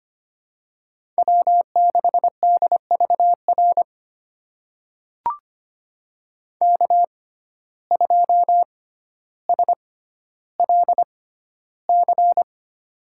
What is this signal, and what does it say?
Morse code, heard as a keyed tone, about 25 wpm, spaced as for 13 wpm, 700 Hz. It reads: W6DVR K2SLC
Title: USA Call Signs - 5 Characters - 6x Character Spacing - 25wpm